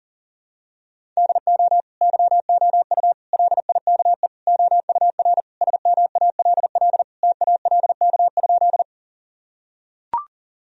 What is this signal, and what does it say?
Morse code, heard as a keyed tone, about 37 wpm, 700 Hz. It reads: DO YOU LIKE OUR SMALL TALK?
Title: Sentences from Top 1000 Words - Review Part 1 - 40wpm